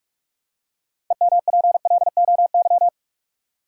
Morse code, heard as a keyed tone, about 46 wpm, 700 Hz. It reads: EMPLOY
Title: Six Letter Words - 45wpm